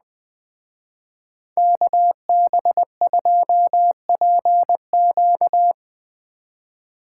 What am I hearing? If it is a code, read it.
Morse code, heard as a keyed tone, about 20 wpm, 700 Hz. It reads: KB2PQ
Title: Speed Racing - USA 2x2 Call Signs 17wpm